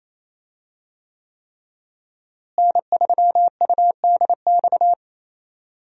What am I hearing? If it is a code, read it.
Morse code, heard as a keyed tone, about 27 wpm, 700 Hz. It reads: N3UDX
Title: USA Call Signs - 5 Characters - Repeated 3x - 28wpm